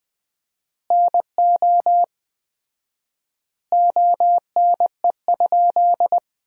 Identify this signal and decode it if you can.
Morse code, heard as a keyed tone, about 19 wpm, 700 Hz. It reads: NO ONE?
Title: 4x Word-Spacing - Sentences from Top 300 Words - 20wpm